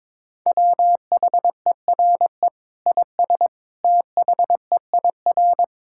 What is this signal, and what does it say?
Morse code, heard as a keyed tone, about 22 wpm, 700 Hz. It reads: WHERE IS THEIR
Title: Sentences from Top 1600 Words 22wpm